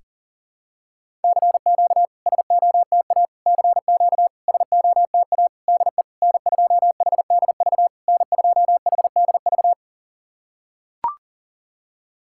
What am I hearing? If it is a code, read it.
Morse code, heard as a keyed tone, about 38 wpm, 700 Hz. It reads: CQ SOTA CQ SOTA DE N2HDV N2HDV
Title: SOTA QSO Practice 40wpm